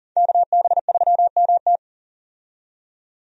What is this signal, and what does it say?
Morse code, heard as a keyed tone, about 40 wpm, 700 Hz. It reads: KB3MT
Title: USA Call Signs - 5 Characters - 40wpm